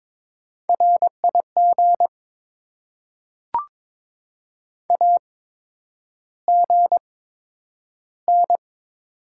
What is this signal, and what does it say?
Morse code, heard as a keyed tone, about 22 wpm, spaced as for 9 wpm, 700 Hz. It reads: RIG AGN
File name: Top 20 QSO Elements - 8x Character Spacing - 22wpm